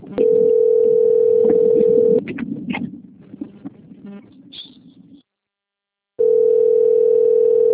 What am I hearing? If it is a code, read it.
Ringback tone (North America), 440 + 480 Hz.
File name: rg-1600-unknown- - - .60017